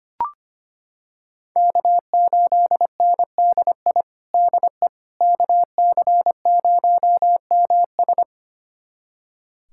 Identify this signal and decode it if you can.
Morse code, heard as a keyed tone, about 25 wpm, 700 Hz. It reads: K8NDI DE KC0MS
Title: USA Callsign DE Callsign - 5 Characters - 25wpm